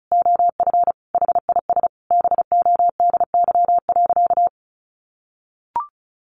Morse code, heard as a keyed tone, about 34 wpm, 700 Hz. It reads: OF HIS BODY.
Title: A Sound Of Thunder 35wpm